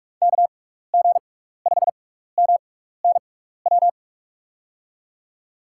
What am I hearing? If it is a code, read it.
Morse code, heard as a keyed tone, about 44 wpm, 700 Hz. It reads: K G 5 M N W
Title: USA Call Signs - 6 Characters - 6x Character Spacing - 45wpm